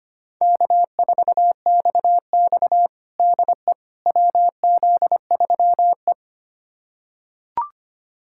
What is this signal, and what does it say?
Morse code, heard as a keyed tone, about 25 wpm, 700 Hz. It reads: K4XX DE WZ3E
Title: USA Callsign DE Callsign - 4 Characters - 25wpm